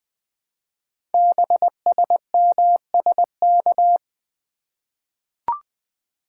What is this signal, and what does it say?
Morse code, heard as a keyed tone, about 20 wpm, 700 Hz. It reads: BSMSK
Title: Code Groups - 5 Letters - 20wpm